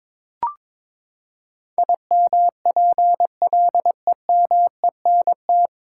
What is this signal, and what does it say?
Morse code, heard as a keyed tone, about 22 wpm, 700 Hz. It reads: IMPLEMENT
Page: Morse Code Ninja practice 22wpm